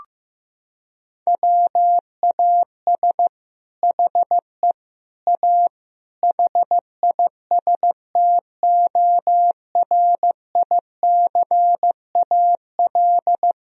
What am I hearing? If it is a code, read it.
Morse code, heard as a keyed tone, about 16 wpm, 700 Hz. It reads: WAS HE A HISTORICAL
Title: Sentences from Top 1000 Words - Review Part 4 - 15wpm